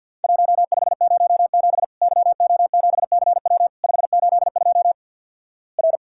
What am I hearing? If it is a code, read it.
Morse code, heard as a keyed tone, about 50 wpm, 700 Hz. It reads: 1507 YO7CW 582 R
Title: Worked All Europe DX Contest - QTC Exchange - 50wpm